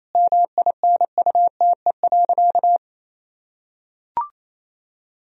Morse code, heard as a keyed tone, about 28 wpm, 700 Hz. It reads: MINUTE.
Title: Sentences from Top 1000 Words - Review Part 1 - 28wpm